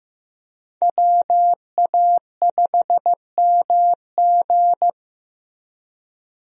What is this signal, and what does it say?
Morse code, heard as a keyed tone, about 15 wpm, 700 Hz. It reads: WA5MG
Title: USA Call Signs - 5 Characters - Repeated 2x - 15wpm